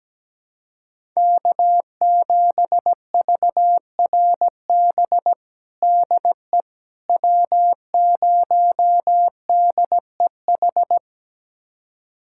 Morse code, heard as a keyed tone, about 17 wpm, 700 Hz. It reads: K7VRB DE W0DEH
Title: USA Callsign DE Callsign - 5 Characters - 17wpm